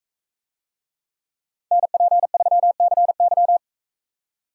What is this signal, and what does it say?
Morse code, heard as a keyed tone, about 43 wpm, 700 Hz. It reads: NP3CY